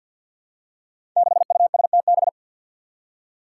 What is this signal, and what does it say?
Morse code, heard as a keyed tone, about 51 wpm, 700 Hz. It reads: 6USTB
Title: Five Challenging Letter-Number Combinations 50wpm